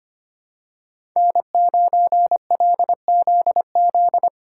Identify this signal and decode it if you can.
Morse code, heard as a keyed tone, about 25 wpm, 700 Hz. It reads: N9LZZ
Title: USA Callsign DE Callsign - 5 Characters - 25wpm